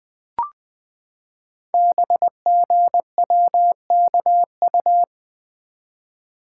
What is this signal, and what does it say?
Morse code, heard as a keyed tone, about 20 wpm, 700 Hz. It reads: BGWKU
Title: Code Groups - 5 Letters - 20wpm